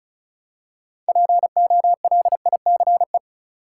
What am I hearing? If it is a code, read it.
Morse code, heard as a keyed tone, about 36 wpm, 700 Hz. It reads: POLICE
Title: Six Letter Words - 35wpm